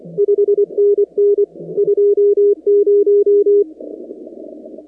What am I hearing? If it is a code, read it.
Morse code, heard as a keyed tone, about 24 wpm, 410 Hz. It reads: 5NN 20
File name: contests / 2019-cw-wpx-kt5j-k5pi- -2330 0-44